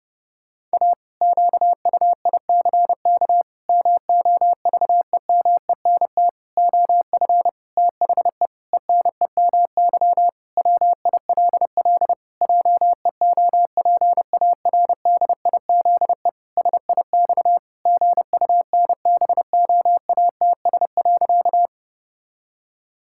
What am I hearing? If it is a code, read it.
Morse code, heard as a keyed tone, about 32 wpm, 700 Hz. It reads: A QUICK MOVEMENT OF THE ENEMY WILL JEOPARDIZE SIX GUNBOATS.
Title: Pangram Challenge v2 - 30wpm